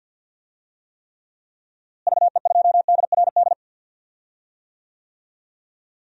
Morse code, heard as a keyed tone, about 52 wpm, 700 Hz. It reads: VE2DRD
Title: DXCC-100 Call Signs - 6 Characters - Repeated 3x - 50wpm